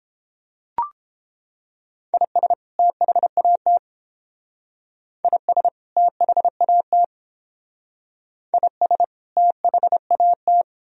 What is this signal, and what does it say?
Morse code, heard as a keyed tone, about 31 wpm, 700 Hz. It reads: IS THAT IS THAT IS THAT